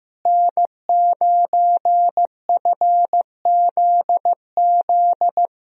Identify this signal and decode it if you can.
Morse code, heard as a keyed tone, about 15 wpm, 700 Hz. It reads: N9FZZ
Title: USA Call Signs - 5 Characters - Repeated 3x - 15wpm